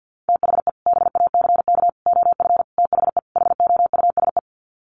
Morse code, heard as a keyed tone, about 50 wpm, 700 Hz. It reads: THE BACK OF THE HOUSE
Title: Sets of 5 Words Encore2 50wpm